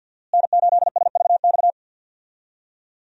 Morse code, heard as a keyed tone, about 50 wpm, 700 Hz. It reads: N8SVX
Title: USA Call Signs - 5 Characters - Repeated 2x - 50wpm